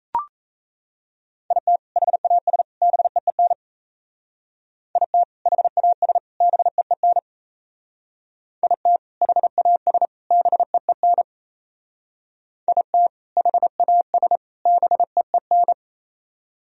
Morse code, heard as a keyed tone, about 34 wpm, 700 Hz. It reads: IT HAS BEEN IT HAS BEEN IT HAS BEEN IT HAS BEEN
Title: Speed-Racing - Sets of 3 Words 28wpm